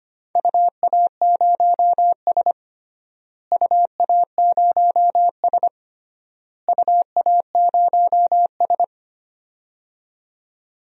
Morse code, heard as a keyed tone, about 25 wpm, 700 Hz. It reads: UA0S UA0S UA0S
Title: DXCC-100 Call Signs - 4 Characters - Repeated 3x - 25wpm